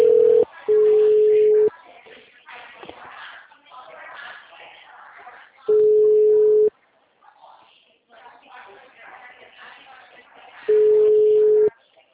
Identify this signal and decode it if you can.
Ringback tone (Europe, 425 Hz), with speech.